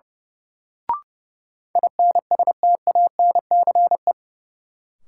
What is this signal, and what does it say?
Morse code, heard as a keyed tone, about 32 wpm, 700 Hz. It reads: INSTANCE
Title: Sentences from Top 800 Words - New Word Drill - 30wpm